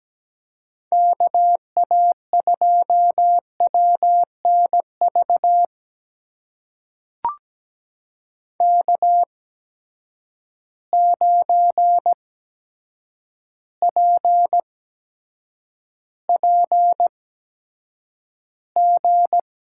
Morse code, heard as a keyed tone, about 17 wpm, spaced as for 8 wpm, 700 Hz. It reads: KA2WNV K9PPG
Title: USA Call Signs - 8x Character Spacing - 17wpm